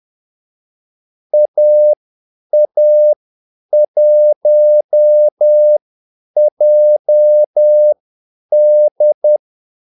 Morse code, heard as a keyed tone, about 10 wpm, spaced as for 8 wpm, 600 Hz. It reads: AA1JD
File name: Callsigns-10-8wpm